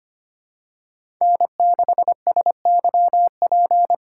Morse code, heard as a keyed tone, about 25 wpm, 700 Hz. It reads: N6SYP
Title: USA Call Signs - 3x Character Spacing - 25wpm